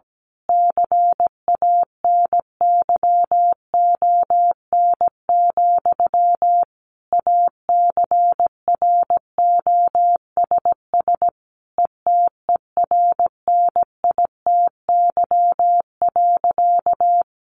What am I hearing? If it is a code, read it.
Morse code, heard as a keyed tone, about 17 wpm, 700 Hz. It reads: CANYON, ACROSS ETERNITY.
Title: A Sound Of Thunder 17wpm